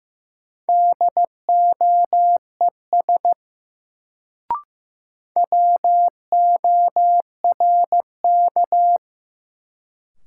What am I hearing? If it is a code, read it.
Morse code, heard as a keyed tone, about 15 wpm, 700 Hz. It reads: DOES WORK
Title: Sentences from Top 200 Words - New Word Drill - 15wpm